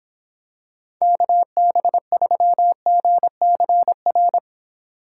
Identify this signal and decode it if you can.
Morse code, heard as a keyed tone, about 26 wpm, 700 Hz. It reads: KB3GCR